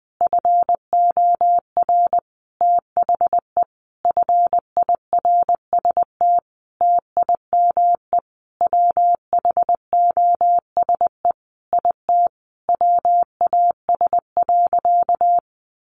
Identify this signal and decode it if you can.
Morse code, heard as a keyed tone, about 20 wpm, 700 Hz. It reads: FOR THE FIRST TIME WHOSE IT WAS.
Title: Sentences from Top 1200 Words 20wpm